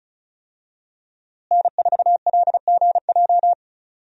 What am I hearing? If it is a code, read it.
Morse code, heard as a keyed tone, about 34 wpm, 700 Hz. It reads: N4LGJ
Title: USA Call Signs - 5 Characters - Repeated 3x - 35wpm